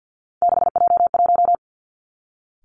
Morse code, heard as a keyed tone, about 50 wpm, 700 Hz. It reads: 6J1
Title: Three Challenging Letter-Number Combinations 50wpm